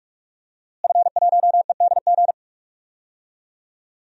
Morse code, heard as a keyed tone, about 42 wpm, 700 Hz. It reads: F1EDG
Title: DXCC-100 Call Signs - 5 Characters - Repeated 2x - 45wpm